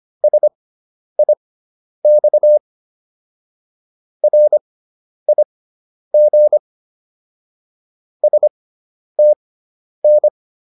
Morse code, heard as a keyed tone, about 25 wpm, spaced as for 10 wpm, 600 Hz. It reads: SIX RIG STN